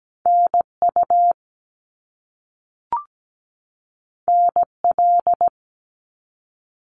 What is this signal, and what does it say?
Morse code, heard as a keyed tone, about 17 wpm, 700 Hz. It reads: NU NL
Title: Canadian Provinces and Territories 17wpm